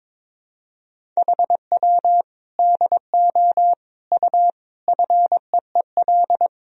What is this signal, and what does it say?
Morse code, heard as a keyed tone, about 22 wpm, 700 Hz. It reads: HW DO U FEEL
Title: Sentences from Top 400 Words and Abbreviations 22wpm